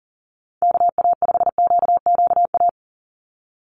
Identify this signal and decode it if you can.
Morse code, heard as a keyed tone, about 37 wpm, 700 Hz. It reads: KA5QQA
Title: NA SOTA Chaser Callsigns 40wpm